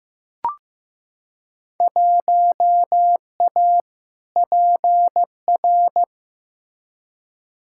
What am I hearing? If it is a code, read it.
Morse code, heard as a keyed tone, about 15 wpm, 700 Hz. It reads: 1A PR
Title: Field Day Exchanges - 15wpm